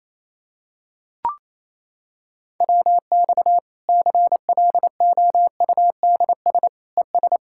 Morse code, heard as a keyed tone, about 28 wpm, 700 Hz. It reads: WX CLOUDS ES
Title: Practice QSO - Exchange 2 - Info - 28wpm